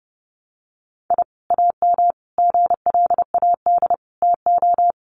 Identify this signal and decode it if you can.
Morse code, heard as a keyed tone, about 30 wpm, 700 Hz. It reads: I AM GLAD TO
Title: Sentences from Top 1300 Words 30wpm